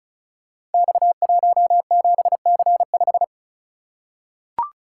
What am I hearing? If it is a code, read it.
Morse code, heard as a keyed tone, about 36 wpm, 700 Hz. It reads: X17C5
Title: Code Groups - 5 Letter:Numbers - 35wpm